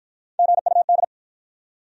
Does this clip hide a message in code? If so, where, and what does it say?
Yes: Morse code from 0.4 to 1.0 s, heard as a keyed tone, about 52 wpm, 700 Hz. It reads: GUD